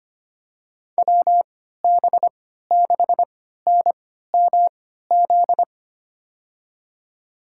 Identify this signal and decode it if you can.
Morse code, heard as a keyed tone, about 25 wpm, spaced as for 14 wpm, 700 Hz. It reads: WB6NMZ
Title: USA Call Signs - 6 Characters - 3x Character Spacing - 25wpm